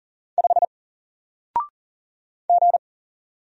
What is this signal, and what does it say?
Morse code, heard as a keyed tone, about 40 wpm, 700 Hz. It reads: 5 G
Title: Single Character (Letter:Number:Punctuation:Prosign) Alt 40wpm